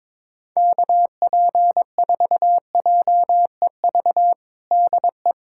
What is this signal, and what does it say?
Morse code, heard as a keyed tone, about 22 wpm, 700 Hz. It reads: KP4JEV DE
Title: USA Callsign DE Callsign 22wpm